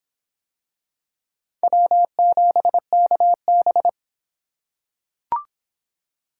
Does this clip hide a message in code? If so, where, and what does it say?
Yes: Morse code from 1.6 to 3.9 s, heard as a keyed tone, about 23 wpm, 700 Hz. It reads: W7KB